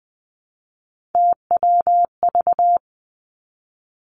Morse code, heard as a keyed tone, about 19 wpm, 700 Hz. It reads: TWV
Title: Three Challenging Letter-Number Combinations 20wpm